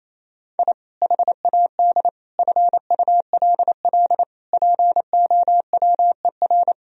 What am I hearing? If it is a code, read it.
Morse code, heard as a keyed tone, about 28 wpm, 700 Hz. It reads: I HAD FULL POWER
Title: Sentences from Top 1000 Words - Review Part 3 - 28wpm